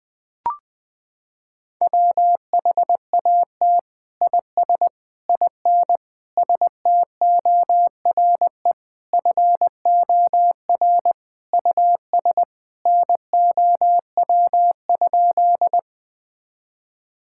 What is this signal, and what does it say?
Morse code, heard as a keyed tone, about 20 wpm, 700 Hz. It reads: WHAT IS IN STORE FOR US NOW?
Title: Sentences from Top 1000 Words - Review Part 5 - 20wpm